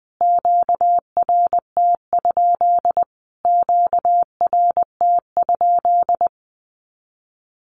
Morse code, heard as a keyed tone, about 20 wpm, 700 Hz. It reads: QRT? QRT?
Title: Q-Codes Repeated 3x 20wpm